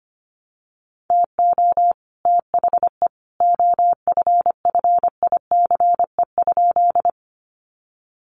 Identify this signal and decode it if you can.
Morse code, heard as a keyed tone, about 25 wpm, 700 Hz. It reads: TO THE OFFICE?
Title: Sentences from Top 1000 Words 25wpm